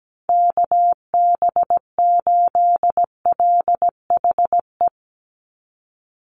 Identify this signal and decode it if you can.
Morse code, heard as a keyed tone, about 17 wpm, 700 Hz. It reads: KB8LHE
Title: USA Call Signs 17wpm